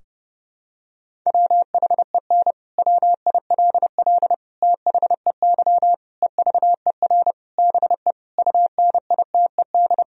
Morse code, heard as a keyed tone, about 30 wpm, 700 Hz. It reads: WHEN WILL THEY EVER BE UNITED
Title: Sentences from Top 1000 Words - Review Part 4 - 30wpm